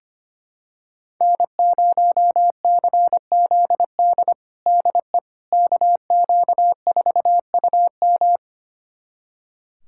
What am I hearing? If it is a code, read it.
Morse code, heard as a keyed tone, about 25 wpm, 700 Hz. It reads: N0CZD DE KQ4UM
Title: USA Callsign DE Callsign - 5 Characters - 25wpm